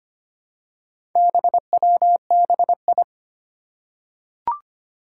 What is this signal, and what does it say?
Morse code, heard as a keyed tone, about 25 wpm, 700 Hz. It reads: BWBI
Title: Four Challenging Letter-Number Combinations 25wpm